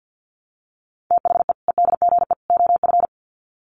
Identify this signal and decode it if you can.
Morse code, heard as a keyed tone, about 50 wpm, 700 Hz. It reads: THE EDGE OF
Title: Sets of 3 Words 50wpm